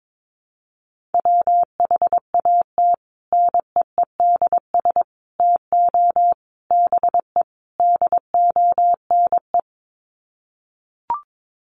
Morse code, heard as a keyed tone, about 22 wpm, 700 Hz. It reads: WHAT NEEDS TO BE DONE
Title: Sets of 5 Words Encore 22wpm